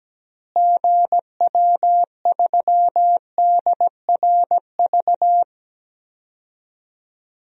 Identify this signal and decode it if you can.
Morse code, heard as a keyed tone, about 17 wpm, 700 Hz. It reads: GW3DRV
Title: DXCC-100 Call Signs - 6 Characters - Repeated 3x - 17wpm